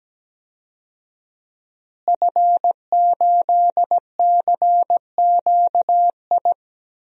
Morse code, heard as a keyed tone, about 17 wpm, 700 Hz. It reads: F8CQI
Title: DXCC-100 Call Signs - 5 Characters - Repeated 3x - 17wpm